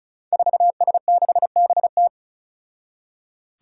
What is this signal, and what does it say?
Morse code, heard as a keyed tone, about 36 wpm, 700 Hz. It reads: 4S6BT
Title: Five Challenging Letter-Number Combinations 35wpm